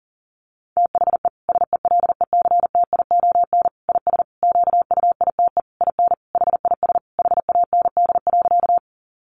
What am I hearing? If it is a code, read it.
Morse code, heard as a keyed tone, about 40 wpm, 700 Hz. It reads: THE SELECTION IS QUITE IN HIS HAND.
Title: Sentences from Top 1400 Words 40wpm